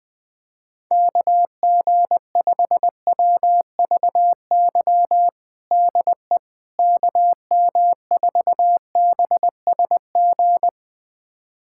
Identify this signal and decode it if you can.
Morse code, heard as a keyed tone, about 20 wpm, 700 Hz. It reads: KG5WVY DE KM4BSG
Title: USA Callsign DE Callsign - 6 Characters - 20wpm